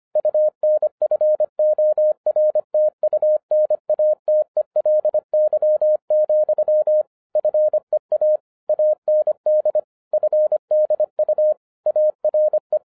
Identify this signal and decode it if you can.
Morse code, heard as a keyed tone, about 26 wpm, 610 Hz. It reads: UNFORTUNATELY, FEA AND FDU ARE
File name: 2012-01-mp3 / n14-FUN WITH NUMBERS-25wpm